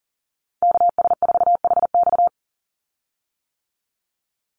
Morse code, heard as a keyed tone, about 40 wpm, 700 Hz. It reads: KS4HX
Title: USA Call Signs 40wpm